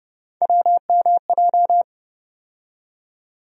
Morse code, heard as a keyed tone, about 30 wpm, 700 Hz. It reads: WMJ
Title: USA Call Sign Postfixes - 3 Letters 30wpm